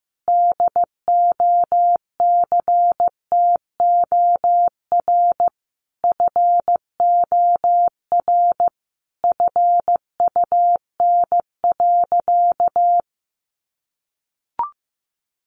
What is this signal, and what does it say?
Morse code, heard as a keyed tone, about 15 wpm, 700 Hz. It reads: DOCTOR FOR FUN.
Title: Additional Easy Phrases 15wpm